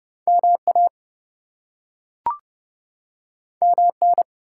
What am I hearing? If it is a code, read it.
Morse code, heard as a keyed tone, about 32 wpm, 700 Hz. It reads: MA MN
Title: US States and Canadian Provinces 30wpm